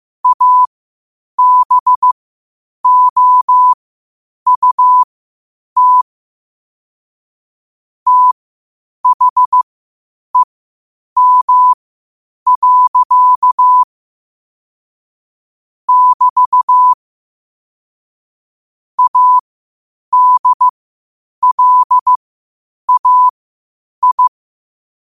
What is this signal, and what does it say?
Morse code, heard as a keyed tone, about 15 wpm, spaced as for 8 wpm, 1000 Hz. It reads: ABOUT THEM. <BT> ADLAI
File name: qotdfiles / 8wpm